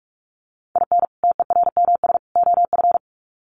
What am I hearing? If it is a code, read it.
Morse code, heard as a keyed tone, about 44 wpm, 700 Hz. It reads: IN TERMS OF
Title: Sets of 3 Words 45wpm